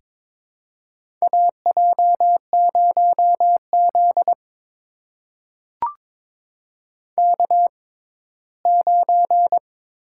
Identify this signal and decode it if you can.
Morse code, heard as a keyed tone, about 22 wpm, spaced as for 15 wpm, 700 Hz. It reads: AJ0Z K9
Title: USA Call Signs - 4 Characters - 6x Character Spacing - 22wpm